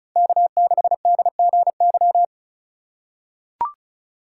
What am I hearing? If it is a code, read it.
Morse code, heard as a keyed tone, about 36 wpm, 700 Hz. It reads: K6DGY
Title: USA Call Signs - 5 Characters - 4x Character Spacing - 35wpm